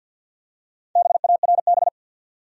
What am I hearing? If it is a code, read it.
Morse code, heard as a keyed tone, about 51 wpm, 700 Hz. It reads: BARB
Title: USA Common Names - 50wpm